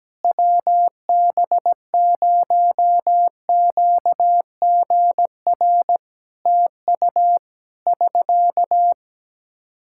Morse code, heard as a keyed tone, about 17 wpm, 700 Hz. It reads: WB0QGR TU <SK>
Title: Practice QSO - Exchange 4 - Ending - 17wpm